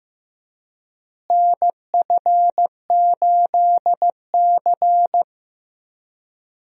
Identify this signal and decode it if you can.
Morse code, heard as a keyed tone, about 15 wpm, 700 Hz. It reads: NF8C
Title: USA Call Signs - 4 Characters - Repeated 3x - 15wpm